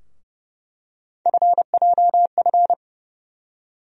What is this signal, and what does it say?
Morse code, heard as a keyed tone, about 30 wpm, 700 Hz. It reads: FJF
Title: USA Call Sign Postfixes - 3 Letters 30wpm